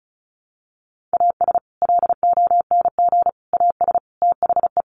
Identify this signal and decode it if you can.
Morse code, heard as a keyed tone, about 34 wpm, 700 Hz. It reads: AS LONG AS THE
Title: Sets of 4 Words 35wpm